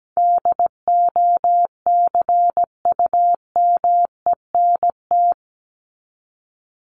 Morse code, heard as a keyed tone, about 17 wpm, 700 Hz. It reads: DOCUMENT
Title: Top 1751-2000 Words 17wpm